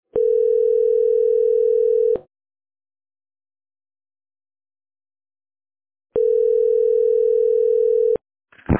Ringback tone (North America), 440 + 480 Hz.